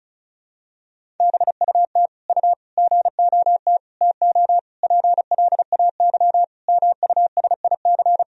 Morse code, heard as a keyed tone, about 36 wpm, 700 Hz. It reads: BUT U GOT TO PLAY MUSIC
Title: Sentences from Top 300 Words and Abbreviations 35wpm